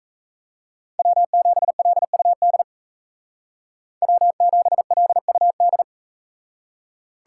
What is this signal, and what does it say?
Morse code, heard as a keyed tone, about 42 wpm, 700 Hz. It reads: W7LUD W7LUD